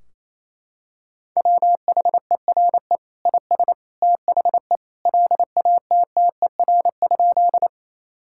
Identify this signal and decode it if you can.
Morse code, heard as a keyed tone, about 28 wpm, 700 Hz. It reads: WHERE IS THE LATTER?
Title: Sentences from Top 1600 Words 28wpm